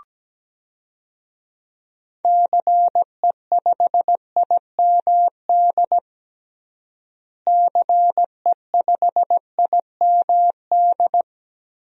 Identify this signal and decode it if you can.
Morse code, heard as a keyed tone, about 17 wpm, 700 Hz. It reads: CE5IMD CE5IMD
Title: DXCC-100 Call Signs - 6 Characters - Repeated 2x - 17wpm